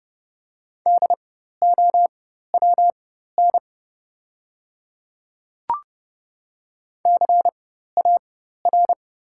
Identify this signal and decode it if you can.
Morse code, heard as a keyed tone, about 30 wpm, spaced as for 14 wpm, 700 Hz. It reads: DOWN CAR
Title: 4x Character Spacing - Top 300 Words - 30wpm